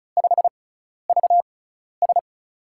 Morse code, heard as a keyed tone, about 36 wpm, spaced as for 12 wpm, 700 Hz. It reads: 5VS
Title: USA Call Signs - 6x Character Spacing - 35wpm